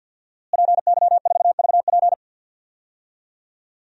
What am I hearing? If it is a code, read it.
Morse code, heard as a keyed tone, about 47 wpm, 700 Hz. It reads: PY4VP
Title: DXCC-100 Call Signs - 5 Characters - Repeated 3x - 50wpm